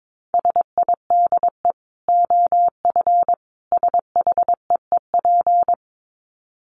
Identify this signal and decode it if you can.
Morse code, heard as a keyed tone, about 22 wpm, 700 Hz. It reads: SIDE OF SHEEP
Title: Dit Challenge 22wpm